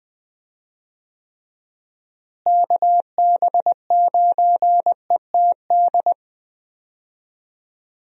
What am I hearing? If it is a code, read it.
Morse code, heard as a keyed tone, about 20 wpm, 700 Hz. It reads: KB9ETD
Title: USA Call Signs - 6 Characters - Repeated 3x - 20wpm